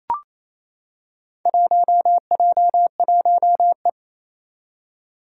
Morse code, heard as a keyed tone, about 27 wpm, 700 Hz. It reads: 1J1E
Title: Four Challenging Letter-Number Combinations 28wpm